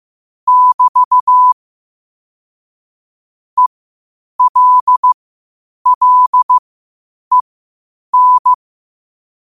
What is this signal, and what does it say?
Morse code, heard as a keyed tone, about 15 wpm, spaced as for 8 wpm, 1000 Hz. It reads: <BT> ELLEN
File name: qotdfiles / 8wpm